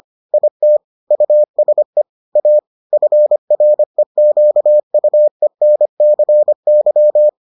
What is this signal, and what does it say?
Morse code, heard as a keyed tone, about 25 wpm, 600 Hz. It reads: IT USE A FREQUENCY